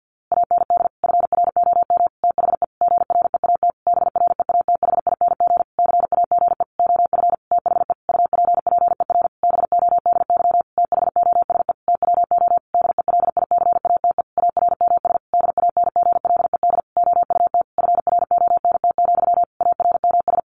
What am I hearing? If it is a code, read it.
Morse code, heard as a keyed tone, about 50 wpm, 700 Hz. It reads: AND FROM THE GREAT BREATHING CAGE OF THE UPPER BODY THOSE TWO DELICATE ARMS DANGLED OUT FRONT, ARMS
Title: A Sound Of Thunder 50wpm